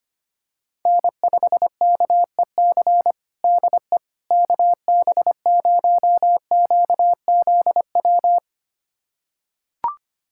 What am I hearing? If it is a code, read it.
Morse code, heard as a keyed tone, about 25 wpm, 700 Hz. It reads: N5KEC DE KB0QZW
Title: USA Callsign DE Callsign 25wpm